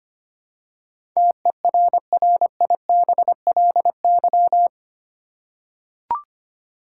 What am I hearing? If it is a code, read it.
Morse code, heard as a keyed tone, about 25 wpm, 700 Hz. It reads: TERRIBLY
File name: Eight Letter Words - 25wpm